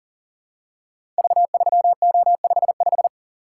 Morse code, heard as a keyed tone, about 39 wpm, 700 Hz. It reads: V3O55
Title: Code Groups - 5 Letter:Numbers - 40wpm